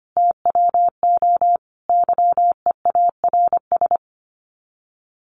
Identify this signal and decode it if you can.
Morse code, heard as a keyed tone, about 25 wpm, 700 Hz. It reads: TWO YEARS
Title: Sets of 5 Words Encore2 25wpm